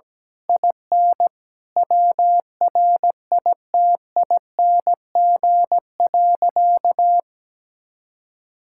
Morse code, heard as a keyed tone, about 17 wpm, 700 Hz. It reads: IN WRITING.
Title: Sentences from Top 1600 Words 17wpm